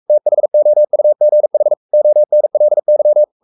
Morse code, heard as a keyed tone, about 44 wpm, 600 Hz. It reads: THOUGH ONLY
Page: Romanen Never af Ken Follet på engelsk.